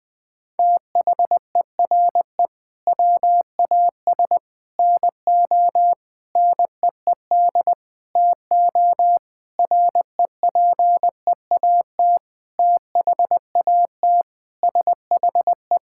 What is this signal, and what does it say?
Morse code, heard as a keyed tone, about 20 wpm, 700 Hz. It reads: THERE WAS NO NEED TO REPEAT THAT SHE
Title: Sentences from Top 1600 Words 20wpm